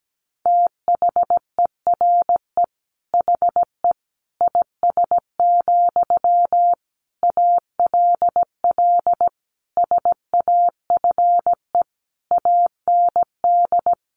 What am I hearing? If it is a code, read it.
Morse code, heard as a keyed tone, about 17 wpm, 700 Hz. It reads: THERE HE IS, ALL SAFE AND
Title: Sentences from Top 900 Words 17wpm